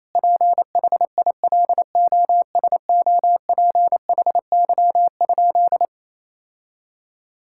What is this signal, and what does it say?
Morse code, heard as a keyed tone, about 28 wpm, 700 Hz. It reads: PHILOSOPHY?
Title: Sentences from Top 1000 Words - Review Part 6 - 28wpm